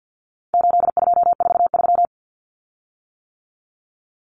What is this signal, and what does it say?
Morse code, heard as a keyed tone, about 50 wpm, 700 Hz. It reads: 8243
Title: Common Cut Numbers - 4-Digit 50wpm